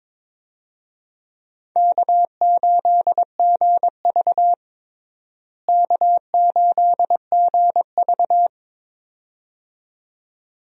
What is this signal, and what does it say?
Morse code, heard as a keyed tone, about 22 wpm, 700 Hz. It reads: K8GV K8GV
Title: USA Call Signs - 4 Characters - Repeated 2x - 22wpm